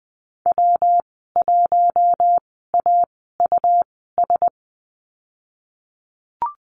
Morse code, heard as a keyed tone, about 20 wpm, spaced as for 15 wpm, 700 Hz. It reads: W1AUS
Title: 161 Lesson 26g - Letter g - Call Signs 20wpm